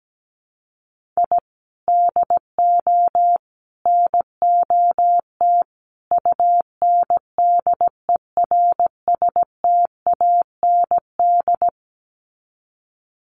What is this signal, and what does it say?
Morse code, heard as a keyed tone, about 17 wpm, 700 Hz. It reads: I DO NOT UNDERSTAND
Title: Sets of 4 Words 17wpm